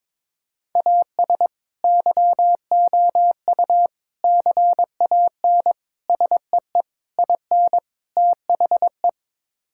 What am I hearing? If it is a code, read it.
Morse code, heard as a keyed tone, about 22 wpm, 700 Hz. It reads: AS YOU CAN SEE IN THE
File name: practice / 22wpm / Sets of 6 Words - 22wpm